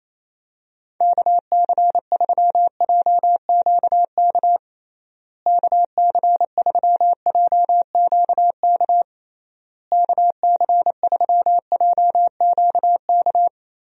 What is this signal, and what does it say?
Morse code, heard as a keyed tone, about 28 wpm, 700 Hz. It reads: KC3JQK KC3JQK KC3JQK
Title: USA Call Signs - 6 Characters - Repeated 3x - 28wpm